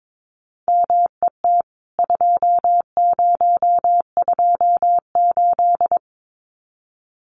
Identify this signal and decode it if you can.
Morse code, heard as a keyed tone, about 22 wpm, 700 Hz. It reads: MET 2028
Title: CWT Exchange 22wpm